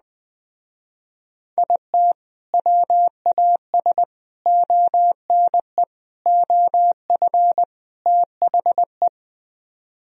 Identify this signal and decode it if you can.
Morse code, heard as a keyed tone, about 20 wpm, 700 Hz. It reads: IT WAS ONE OF THE